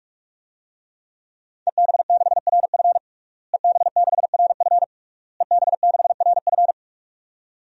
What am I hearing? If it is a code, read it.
Morse code, heard as a keyed tone, about 46 wpm, 700 Hz. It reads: EB6RF EB6RF EB6RF
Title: DXCC-100 Call Signs - 5 Characters - Repeated 3x - 45wpm